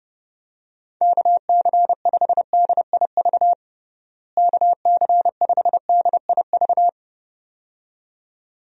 Morse code, heard as a keyed tone, about 30 wpm, 700 Hz. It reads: KC5DIV KC5DIV
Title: USA Call Signs - 6 Characters - Repeated 2x - 30wpm